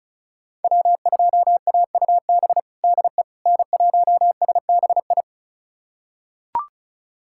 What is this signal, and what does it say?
Morse code, heard as a keyed tone, about 34 wpm, 700 Hz. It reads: W2AUB DE N1SBI
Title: USA Callsign DE Callsign 35wpm